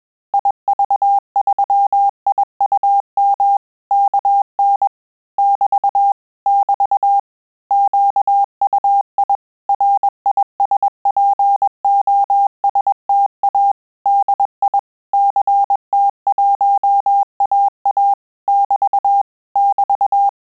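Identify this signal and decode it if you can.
Morse code, heard as a keyed tone, about 21 wpm, 800 Hz. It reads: IV3IUM KN <BT> <BT> QUI RISPOSTA DI CT1AA <BT> <BT>
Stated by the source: Morse QSO practice at 20 words per minute, in Italian and English.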